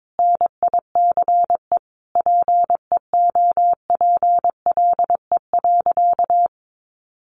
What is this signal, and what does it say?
Morse code, heard as a keyed tone, about 22 wpm, 700 Hz. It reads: NICE PEOPLE.
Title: Sentences from Top 1100 Words 22wpm